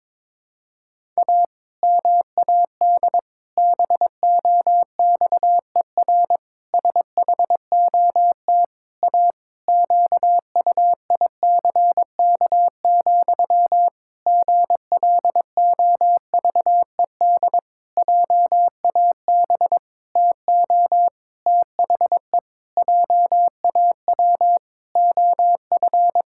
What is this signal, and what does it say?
Morse code, heard as a keyed tone, about 22 wpm, 700 Hz. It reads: A MAD BOXER SHOT A QUICK, GLOVED JAB TO THE JAW OF
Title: Pangram Challenge v2 - 22wpm